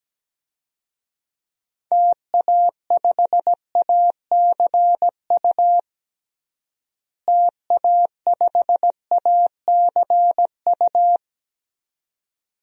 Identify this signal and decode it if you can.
Morse code, heard as a keyed tone, about 17 wpm, 700 Hz. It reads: TA5ACU TA5ACU
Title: DXCC-100 Call Signs - 6 Characters - Repeated 2x - 17wpm